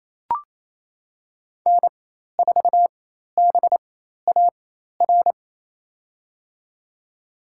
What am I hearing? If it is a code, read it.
Morse code, heard as a keyed tone, about 28 wpm, spaced as for 13 wpm, 700 Hz. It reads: N4BAR
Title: USA Call Signs - 4x Character Spacing - 28wpm